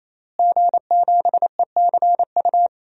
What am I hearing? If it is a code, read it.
Morse code, heard as a keyed tone, about 28 wpm, 700 Hz. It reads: G7ECU
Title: DXCC-100 Call Signs - 5 Characters - Repeated 3x - 28wpm